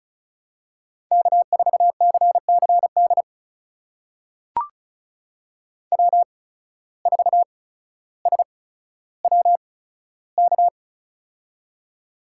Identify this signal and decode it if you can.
Morse code, heard as a keyed tone, about 36 wpm, spaced as for 15 wpm, 700 Hz. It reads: K4CCD W4SWK
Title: USA Call Signs - 5 Characters - 8x Character Spacing - 35wpm